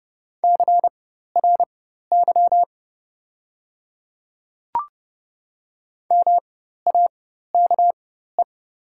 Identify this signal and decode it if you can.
Morse code, heard as a keyed tone, about 30 wpm, spaced as for 14 wpm, 700 Hz. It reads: CRY MAKE
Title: 4x Character Spacing - Top 500 Words - 30wpm